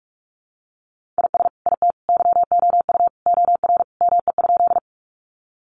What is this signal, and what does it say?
Morse code, heard as a keyed tone, about 44 wpm, 700 Hz. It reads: IS IT YOU OR ME?
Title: Sentences from Top 200 Words 45wpm